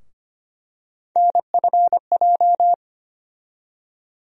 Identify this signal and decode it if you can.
Morse code, heard as a keyed tone, about 25 wpm, 700 Hz. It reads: NFJ
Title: USA Call Sign Postfixes - 3 Letters 25wpm